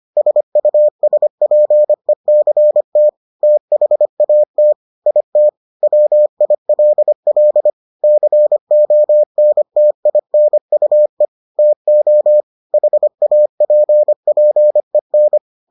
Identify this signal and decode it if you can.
Morse code, heard as a keyed tone, about 25 wpm, 600 Hz. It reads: SUSPECT THAT IT WILL CONTINUE TO HAPPEN